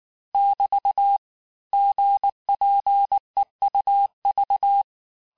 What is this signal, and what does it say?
Morse code, heard as a keyed tone, about 19 wpm, 800 Hz. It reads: <BT> GPEUV
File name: cours-du-lundi files / cours-1589-texte-chiffre-vitesse-18mpm